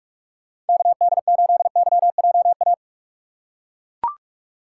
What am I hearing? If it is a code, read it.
Morse code, heard as a keyed tone, about 42 wpm, 700 Hz. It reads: KD8YJA
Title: USA Call Signs - 6 Characters - 4x Character Spacing - 45wpm